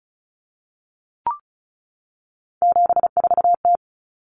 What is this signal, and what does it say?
Morse code, heard as a keyed tone, about 34 wpm, 700 Hz. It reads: 74T
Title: Common Cut Numbers - 3-Digit 35wpm